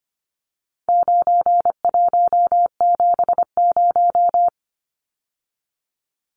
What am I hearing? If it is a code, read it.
Morse code, heard as a keyed tone, about 25 wpm, 700 Hz. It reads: 9170
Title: Numbers - 4-Digit 25wpm